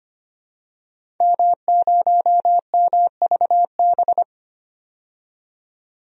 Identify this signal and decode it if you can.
Morse code, heard as a keyed tone, about 25 wpm, 700 Hz. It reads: M0MVB
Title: DXCC-100 Call Signs - 5 Characters - 25wpm